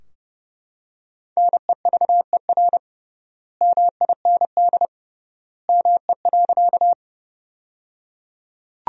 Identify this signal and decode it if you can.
Morse code, heard as a keyed tone, about 30 wpm, 700 Hz. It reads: NEVER MIND ME.
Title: 3x Word-Spacing - Sentences from Top 500 Words - 30wpm